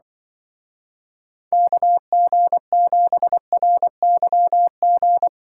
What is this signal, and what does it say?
Morse code, heard as a keyed tone, about 25 wpm, 700 Hz. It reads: KG7RYG
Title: Speed Racing - USA Call Signs 20wpm